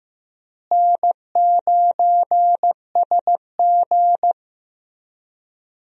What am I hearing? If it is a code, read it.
Morse code, heard as a keyed tone, about 15 wpm, 700 Hz. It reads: N9SG
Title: USA Call Signs - 4 Characters - Repeated 3x - 15wpm